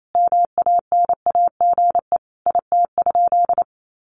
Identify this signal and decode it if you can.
Morse code, heard as a keyed tone, about 28 wpm, 700 Hz. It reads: MANAGE IT?
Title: Sentences from Top 1400 Words 28wpm